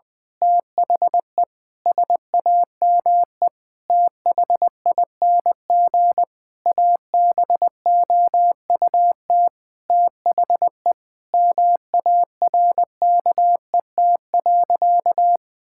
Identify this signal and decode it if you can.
Morse code, heard as a keyed tone, about 20 wpm, 700 Hz. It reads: THE SAME THING ABOUT THE MARKET.
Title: Sentences from Top 1000 Words - Review Part 5 - 20wpm